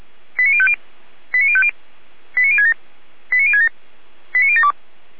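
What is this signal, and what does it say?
Five-tone selective call (ZVEI), heard as quick runs of five tones, tones of about 70 ms. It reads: 89050 89050 89067 89067 89072
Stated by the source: Five-tone ZVEI selective call received on an SDR.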